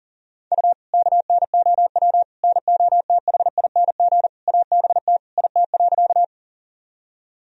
Morse code, heard as a keyed tone, about 40 wpm, 700 Hz. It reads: U KNOW NOTHING ABT IT.
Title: Sentences from Top 500 Words and Abbreviations 40wpm